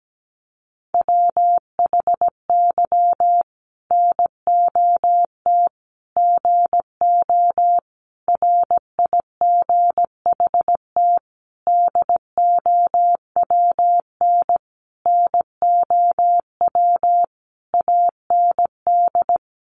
Morse code, heard as a keyed tone, about 17 wpm, 700 Hz. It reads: WHY NOT GO RIGHT DOWN NOW AND
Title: Sentences from Top 1400 Words 17wpm